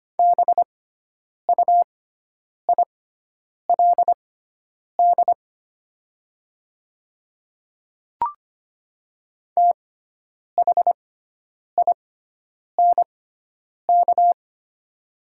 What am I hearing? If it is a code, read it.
Morse code, heard as a keyed tone, about 25 wpm, spaced as for 9 wpm, 700 Hz. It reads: BUILD THINK
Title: 6x Character Spacing - Top 300 Words - 25wpm